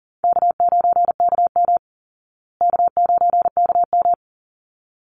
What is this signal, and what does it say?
Morse code, heard as a keyed tone, about 40 wpm, 700 Hz. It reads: K9KM K9KM
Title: Call Signs Repeated 3x - 40wpm